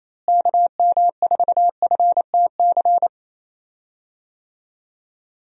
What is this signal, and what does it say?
Morse code, heard as a keyed tone, about 27 wpm, 700 Hz. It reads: KM4FTC
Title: USA Call Signs - 6 Characters - Repeated 3x - 28wpm